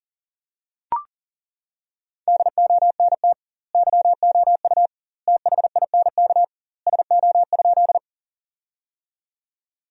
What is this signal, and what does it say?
Morse code, heard as a keyed tone, about 40 wpm, 700 Hz. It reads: DONT YOU THINK SO?
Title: 2x Word-Spacing - Sentences from Top 300 Words - 40wpm